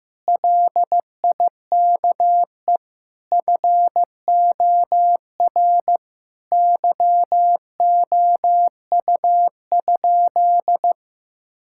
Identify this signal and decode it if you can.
Morse code, heard as a keyed tone, about 15 wpm, 700 Hz. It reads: LIKE FOR YOU?
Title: Sentences from Top 1000 Words - Review Part 6 - 15wpm